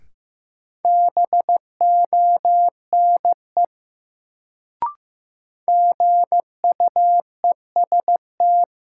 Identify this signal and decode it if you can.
Morse code, heard as a keyed tone, about 16 wpm, 700 Hz. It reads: BONE GUEST
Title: Sentences from Top 1600 Words - New Word Drill - 15wpm